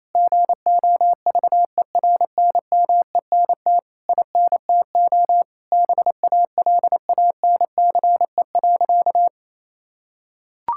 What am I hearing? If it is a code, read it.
Morse code, heard as a keyed tone, about 27 wpm, 700 Hz. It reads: GOVERNMENT INTO BALANCE.
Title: Sentences from Top 1000 Words - Review Part 6 - 28wpm